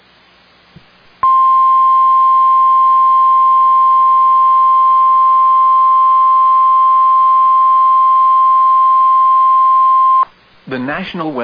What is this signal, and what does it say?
Emergency Alert System (SAME) tones (the 1050 Hz attention tone), with speech. It is heard from 1.2 to 10.3 s.